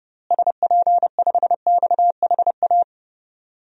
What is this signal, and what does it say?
Morse code, heard as a keyed tone, about 30 wpm, 700 Hz. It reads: SP5XHA
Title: DXCC-100 Call Signs - 6 Characters - Repeated 2x - 30wpm